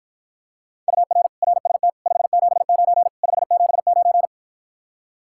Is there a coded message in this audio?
Yes: Morse code from 0.9 to 4.3 s, heard as a keyed tone, about 54 wpm, 700 Hz. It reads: UR RST 579 579